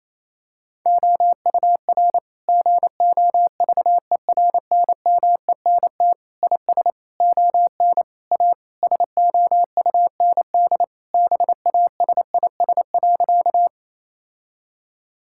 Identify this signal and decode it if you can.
Morse code, heard as a keyed tone, about 28 wpm, 700 Hz. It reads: OUR GOVERNMENT IS ON A SOUND BASIS.
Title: Sentences from Top 1000 Words - Review Part 5 - 28wpm